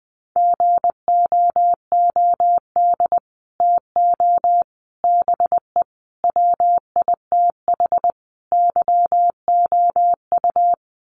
Morse code, heard as a keyed tone, about 20 wpm, 700 Hz. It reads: GOOD TO BE WITH YOU
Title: Sets of 5 Words Encore2 20wpm